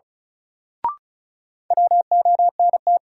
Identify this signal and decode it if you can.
Morse code, heard as a keyed tone, about 36 wpm, 700 Hz. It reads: WONT
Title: Sentences from Top 1000 Words - Review Part 6 - New Word Drill - 35wpm